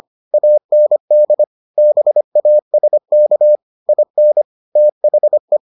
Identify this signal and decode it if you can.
Morse code, heard as a keyed tone, about 25 wpm, 600 Hz. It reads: AND BASK IN THE